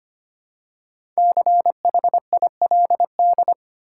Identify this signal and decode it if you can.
Morse code, heard as a keyed tone, about 25 wpm, 700 Hz. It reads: CHILD
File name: 5 Letter Words - First 3 Letters Given - 25wpm